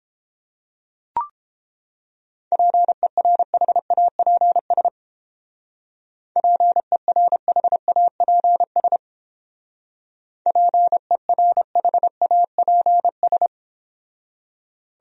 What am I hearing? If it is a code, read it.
Morse code, heard as a keyed tone, about 30 wpm, 700 Hz. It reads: PERHAPS PERHAPS PERHAPS